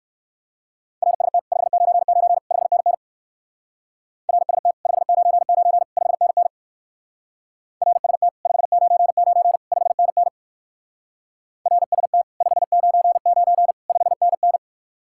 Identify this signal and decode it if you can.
Morse code, heard as a keyed tone, about 59 wpm, 700 Hz. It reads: RST 599 5NN RST 599 5NN RST 599 5NN RST 599 5NN